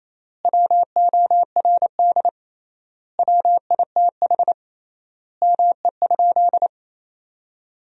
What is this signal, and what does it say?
Morse code, heard as a keyed tone, about 28 wpm, 700 Hz. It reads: WORD WITH ME?
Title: 3x Word-Spacing - Sentences from Top 300 Words - 28wpm